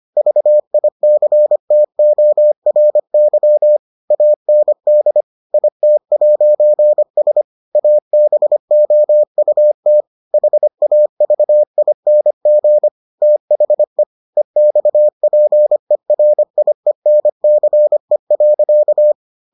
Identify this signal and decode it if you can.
Morse code, heard as a keyed tone, about 25 wpm, 600 Hz. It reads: VICTORY AND IT'S ABOUT HAVING THE EXPERIENCE.